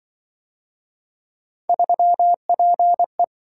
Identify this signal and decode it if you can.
Morse code, heard as a keyed tone, about 25 wpm, 700 Hz. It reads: 3PE
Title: Speed-Racing - Three Letter-Number Combinations - 20wpm